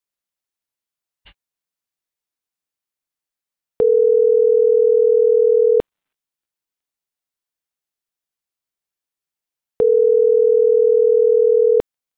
Ringback tone (North America), 440 + 480 Hz.